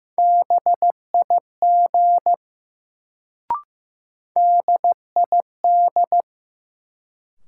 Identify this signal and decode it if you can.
Morse code, heard as a keyed tone, about 15 wpm, 700 Hz. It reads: BIG DID
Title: Sets of 2 Words - New Word Drill - 15wpm